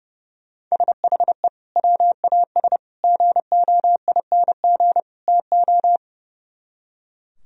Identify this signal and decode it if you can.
Morse code, heard as a keyed tone, about 35 wpm, 700 Hz. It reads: SHE WAS GOING TO
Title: Speed-Racing - Sets of 4 Words 30wpm